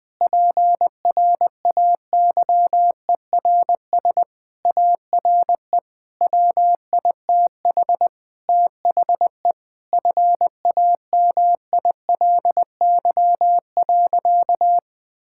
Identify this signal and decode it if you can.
Morse code, heard as a keyed tone, about 19 wpm, 700 Hz. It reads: PRAYERS ARE WITH THE FAMILY.
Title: Sentences with 7-Word Phrases - Spoken Beginning - 20wpm